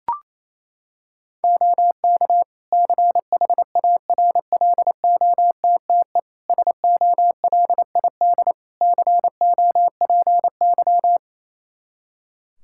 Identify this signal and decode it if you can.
Morse code, heard as a keyed tone, about 28 wpm, 700 Hz. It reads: OK CHARLOTTE SOLID COPY
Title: Practice QSO - Exchange 3 - More Info - 28wpm